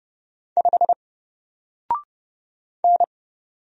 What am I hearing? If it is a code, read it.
Morse code, heard as a keyed tone, about 29 wpm, 700 Hz. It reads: 5 N
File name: Single-Letter-Number-30wpm